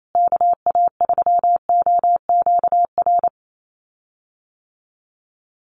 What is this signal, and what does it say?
Morse code, heard as a keyed tone, about 28 wpm, 700 Hz. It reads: KA3OQR
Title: USA Call Signs Repeated 3x 28wpm